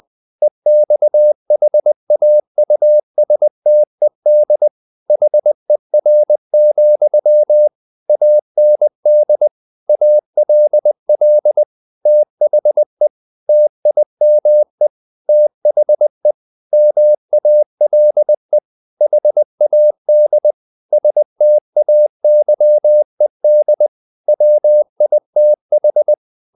Morse code, heard as a keyed tone, about 20 wpm, 600 Hz. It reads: EXHAUSTED HER, AND ALL THE TIME THE MALE HAD STAYED WITH
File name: the old man and the sea / the old man and the sea part 1 20wpm